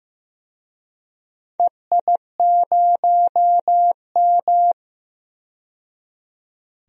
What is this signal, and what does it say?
Morse code, heard as a keyed tone, about 15 wpm, 700 Hz. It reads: EI0M
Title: DXCC-100 Call Signs - 4 Characters - 15wpm